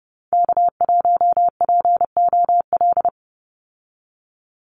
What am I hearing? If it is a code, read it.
Morse code, heard as a keyed tone, about 30 wpm, 700 Hz. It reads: K1POL
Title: USA Call Signs 30wpm